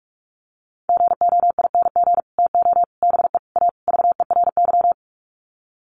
Morse code, heard as a keyed tone, about 44 wpm, 700 Hz. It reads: GOING TO BE A VERY
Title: Sets of 5 Words Encore 45wpm